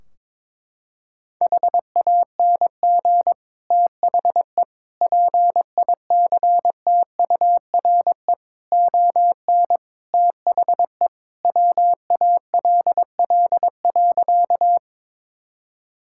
Morse code, heard as a keyed tone, about 22 wpm, 700 Hz. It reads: HANG THE PICTURE ON THE WALL.
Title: Sentences from Top 1600 Words 22wpm